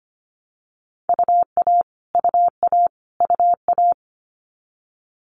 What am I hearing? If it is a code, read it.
Morse code, heard as a keyed tone, about 25 wpm, 700 Hz. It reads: UA UA UA
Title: Multi-letter Phonemes Repeated 3x 25wpm